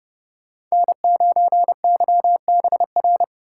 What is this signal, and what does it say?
Morse code, heard as a keyed tone, about 35 wpm, 700 Hz. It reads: N9YBR
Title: Speed Racing - USA 1x3 Call Signs 30wpm